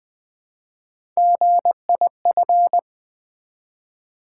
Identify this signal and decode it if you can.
Morse code, heard as a keyed tone, about 20 wpm, 700 Hz. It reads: GIF
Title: Build A Word - 1 Letter At A Time - 4-Letter Words - 20wpm